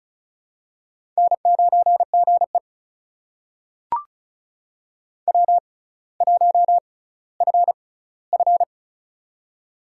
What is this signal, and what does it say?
Morse code, heard as a keyed tone, about 36 wpm, spaced as for 18 wpm, 700 Hz. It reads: N9GE W1FF
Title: USA Call Signs - 4 Characters - 6x Character Spacing - 35wpm